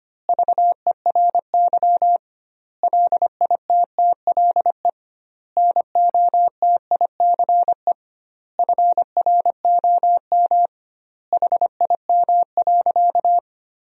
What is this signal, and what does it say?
Morse code, heard as a keyed tone, about 25 wpm, 700 Hz. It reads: VERY LITTLE NOTICE FROM HIM.
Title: 2x Word-Spacing - Sentences from Top 400 Words - 25wpm